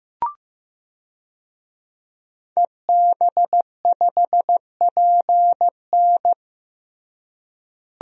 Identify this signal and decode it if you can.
Morse code, heard as a keyed tone, about 15 wpm, 700 Hz. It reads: EB5PN
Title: DXCC-100 Call Signs - 5 Characters - Repeated 2x - 15wpm